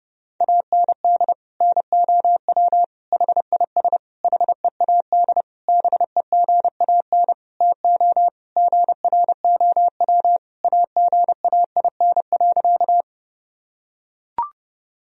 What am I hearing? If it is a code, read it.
Morse code, heard as a keyed tone, about 30 wpm, 700 Hz. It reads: AND NOW HIS HEAD BEGAN TO GROW AGAIN.
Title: Sentences from Top 1000 Words - Review Part 1 - 30wpm